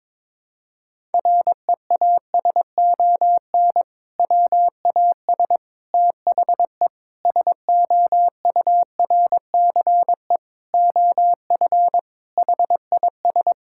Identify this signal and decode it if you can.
Morse code, heard as a keyed tone, about 22 wpm, 700 Hz. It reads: REASON WAS THE SOURCE OF HIS
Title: Sentences from Top 1000 Words - Review Part 5 - 22wpm